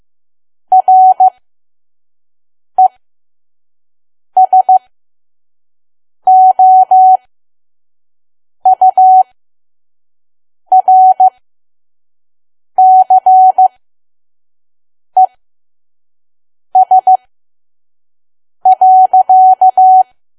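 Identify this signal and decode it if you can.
Morse code, heard as a keyed tone, about 15 wpm, spaced as for 5 wpm, 750 Hz. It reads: RESOURCES.
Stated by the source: Morse code practice at 5 WPM from KQ2P.COM.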